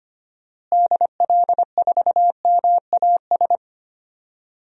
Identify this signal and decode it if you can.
Morse code, heard as a keyed tone, about 25 wpm, 700 Hz. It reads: DL4MAS
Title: DXCC-100 Call Signs - 6 Characters - Repeated 3x - 25wpm